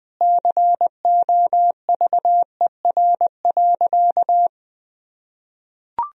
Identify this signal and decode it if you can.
Morse code, heard as a keyed tone, about 20 wpm, 700 Hz. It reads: COVER.
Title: Sentences from Top 1000 Words - Review Part 1 - 20wpm